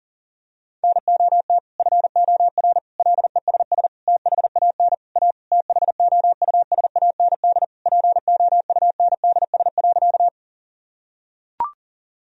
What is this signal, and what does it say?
Morse code, heard as a keyed tone, about 40 wpm, 700 Hz. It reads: NOT FOR LESS THAN A THOUSAND POUNDS.
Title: Sentences from Top 1000 Words - Review Part 3 - 40wpm